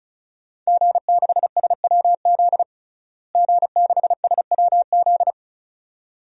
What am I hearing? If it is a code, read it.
Morse code, heard as a keyed tone, about 36 wpm, 700 Hz. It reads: G6SWZ G6SWZ
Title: DXCC-100 Call Signs - 5 Characters - Repeated 2x - 35wpm